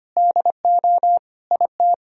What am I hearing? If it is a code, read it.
Morse code, heard as a keyed tone, about 34 wpm, spaced as for 30 wpm, 700 Hz. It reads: DO IT
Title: Speed-Racing - Sets of 3 Words 25wpm